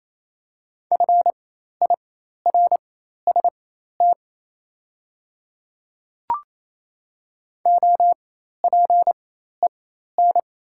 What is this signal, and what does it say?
Morse code, heard as a keyed tone, about 27 wpm, spaced as for 13 wpm, 700 Hz. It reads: FIRST OPEN
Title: 4x Character Spacing - Top 300 Words - 28wpm